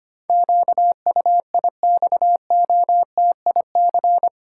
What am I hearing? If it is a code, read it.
Morse code, heard as a keyed tone, about 25 wpm, 700 Hz. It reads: QUIXOTIC
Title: Pangram Challenge v2 - 25wpm